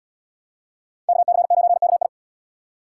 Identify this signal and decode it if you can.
Morse code, heard as a keyed tone, about 73 wpm, 700 Hz. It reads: KC9KI